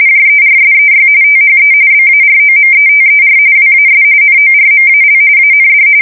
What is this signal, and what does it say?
RTTY, about 45.45 baud. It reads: JERSEY THE BAROMETER WAS NORMAL AND
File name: fsk170 / USS-New-Jersey